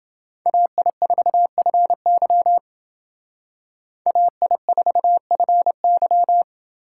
Morse code, heard as a keyed tone, about 30 wpm, 700 Hz. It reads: AI4FY AI4FY